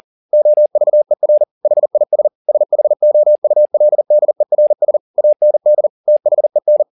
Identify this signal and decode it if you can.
Morse code, heard as a keyed tone, about 40 wpm, 600 Hz. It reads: OVER HIS SHOULDERS AND THEN
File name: the old man and the sea 2 40wpm